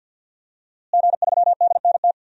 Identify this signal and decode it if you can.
Morse code, heard as a keyed tone, about 46 wpm, 700 Hz. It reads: G3DNT
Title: DXCC-100 Call Signs - 5 Characters - Repeated 3x - 50wpm